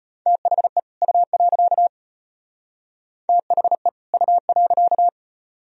Morse code, heard as a keyed tone, about 32 wpm, 700 Hz. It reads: THE U. THE U.